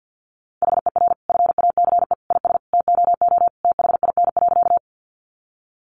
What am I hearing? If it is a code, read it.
Morse code, heard as a keyed tone, about 50 wpm, 700 Hz. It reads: HER FACE IS TOO THIN.
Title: Sentences from Top 1200 Words 50wpm